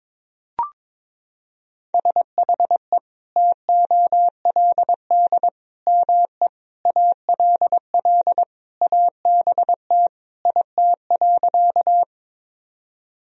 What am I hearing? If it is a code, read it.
Morse code, heard as a keyed tone, about 21 wpm, 700 Hz. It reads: SHE TOLD ME ALL ABT IT.
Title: Sentences from Top 400 Words and Abbreviations 22wpm